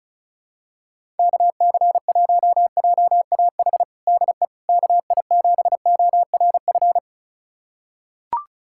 Morse code, heard as a keyed tone, about 36 wpm, 700 Hz. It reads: KC1JAH DE KI7ORF
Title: USA Callsign DE Callsign 35wpm